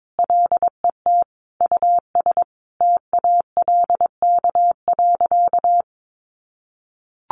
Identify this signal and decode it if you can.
Morse code, heard as a keyed tone, about 22 wpm, 700 Hz. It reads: LET US TALK.
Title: Sentences from Top 1300 Words 22wpm